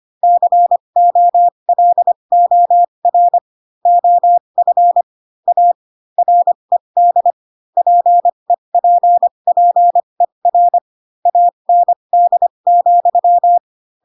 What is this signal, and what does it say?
Morse code, heard as a keyed tone, about 25 wpm, spaced as for 22 wpm, 700 Hz. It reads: COLOR OF A RED PEPPER AND,
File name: Pinocchio / Pinocchio ch2 18 25 ik1hge